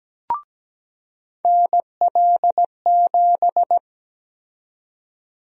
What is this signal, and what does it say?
Morse code, heard as a keyed tone, about 17 wpm, 700 Hz. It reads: NL7
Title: USA Call Sign Prefixes - 17wpm